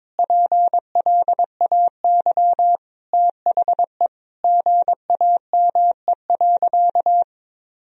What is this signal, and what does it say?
Morse code, heard as a keyed tone, about 22 wpm, 700 Hz. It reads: PLAY THE GAME.
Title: Sentences from Top 1000 Words - Review Part 3 - 22wpm